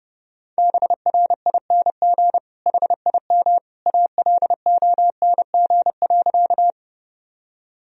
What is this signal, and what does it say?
Morse code, heard as a keyed tone, about 29 wpm, 700 Hz. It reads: BRING HIM ALONG.
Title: Sentences from Top 1000 Words - Review Part 3 - 30wpm